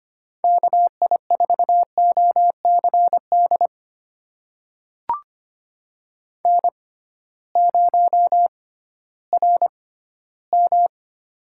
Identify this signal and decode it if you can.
Morse code, heard as a keyed tone, about 25 wpm, spaced as for 15 wpm, 700 Hz. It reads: KI4OCD N0RM
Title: USA Call Signs - 6x Character Spacing - 25wpm